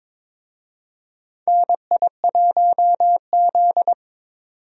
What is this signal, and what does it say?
Morse code, heard as a keyed tone, about 21 wpm, 700 Hz. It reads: NI1Z
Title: USA Call Signs - 4 Characters - 22wpm